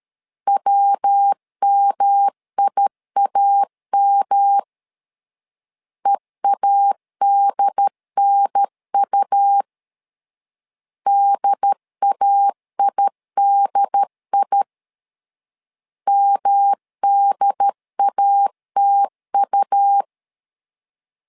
Morse code, heard as a keyed tone, about 13 wpm, 800 Hz. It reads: WMIAM EADNU DAIDI MDATU
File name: LEZIONI / 3 MWRO / 3 MW 12wpm 1633